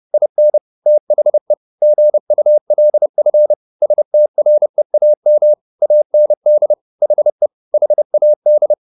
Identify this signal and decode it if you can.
Morse code, heard as a keyed tone, about 30 wpm, 600 Hz. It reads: IN THE GULF STREAM AND HE HAD
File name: the old man and the sea / the old man and the sea 1 30wpm